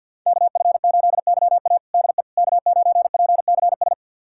Morse code, heard as a keyed tone, about 52 wpm, 700 Hz. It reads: KF8YA DE K9PCS
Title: USA Callsign DE Callsign - 5 Characters - 50wpm